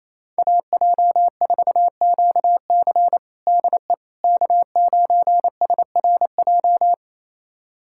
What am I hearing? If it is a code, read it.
Morse code, heard as a keyed tone, about 28 wpm, 700 Hz. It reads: AJ4QC DE K9SRJ
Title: USA Callsign DE Callsign - 5 Characters - 28wpm